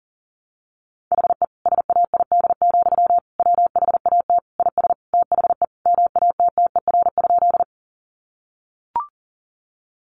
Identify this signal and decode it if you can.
Morse code, heard as a keyed tone, about 40 wpm, 700 Hz. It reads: HE SAID, WHAT IS THE MATTER?
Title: Sentences from Top 700 Words 40wpm